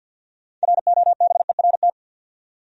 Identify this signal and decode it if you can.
Morse code, heard as a keyed tone, about 52 wpm, 700 Hz. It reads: ROBERT
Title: USA Common Names - 50wpm